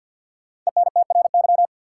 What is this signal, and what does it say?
Morse code, heard as a keyed tone, about 52 wpm, 700 Hz. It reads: ENTRY
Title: Sentences from Top 1500 Words - New Word Drill - 50wpm